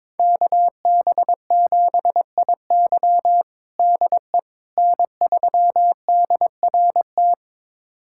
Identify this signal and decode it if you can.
Morse code, heard as a keyed tone, about 22 wpm, 700 Hz. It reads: KB7IY DE N3DRT
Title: USA Callsign DE Callsign - 5 Characters - 22wpm